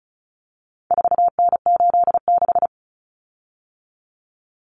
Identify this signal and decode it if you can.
Morse code, heard as a keyed tone, about 34 wpm, 700 Hz. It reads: 4N86
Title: Common Cut Numbers - 4-Digit 35wpm